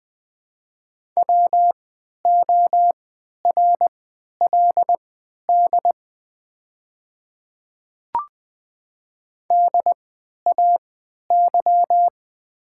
Morse code, heard as a keyed tone, about 20 wpm, spaced as for 11 wpm, 690 Hz. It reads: WORLD DAY
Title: 3x Character Spacing - Top 200 Words - 20wpm